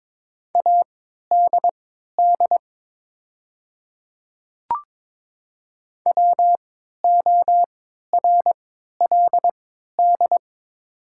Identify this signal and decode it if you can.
Morse code, heard as a keyed tone, about 22 wpm, spaced as for 12 wpm, 700 Hz. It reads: ADD WORLD
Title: 3x Character Spacing - Top 200 Words - 22wpm